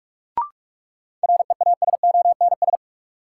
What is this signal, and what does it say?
Morse code, heard as a keyed tone, about 42 wpm, 700 Hz. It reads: REASONS
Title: Sets of 4 Words - New Word Drill - 45wpm